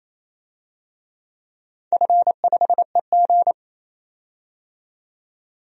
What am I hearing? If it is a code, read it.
Morse code, heard as a keyed tone, about 28 wpm, 700 Hz. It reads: F5EG
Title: DXCC-100 Call Signs - 4 Characters - 28wpm